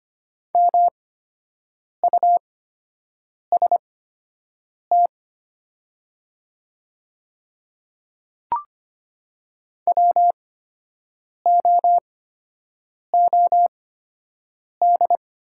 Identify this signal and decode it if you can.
Morse code, heard as a keyed tone, about 25 wpm, spaced as for 7 wpm, 700 Hz. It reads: MUST WOOD
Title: 8x Character Spacing - Top 500 Words - 25wpm